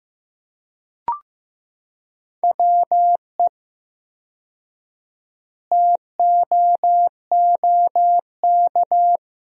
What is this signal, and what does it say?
Morse code, heard as a keyed tone, about 15 wpm, 700 Hz. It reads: WE TOOK
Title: 4x Word-Spacing - Sentences from Top 400 Words - 15wpm